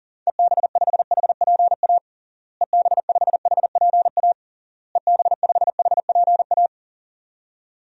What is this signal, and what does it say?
Morse code, heard as a keyed tone, about 40 wpm, 700 Hz. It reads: EB5HPA EB5HPA EB5HPA
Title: DXCC-100 Call Signs - 6 Characters - Repeated 3x - 40wpm